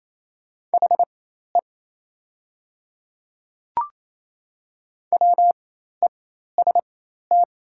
Morse code, heard as a keyed tone, about 28 wpm, spaced as for 13 wpm, 700 Hz. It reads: HE WEST
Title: 4x Character Spacing - Top 500 Words - 28wpm